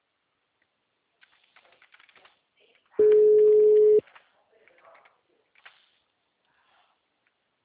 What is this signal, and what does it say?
Ringback tone (Europe, 425 Hz), with speech.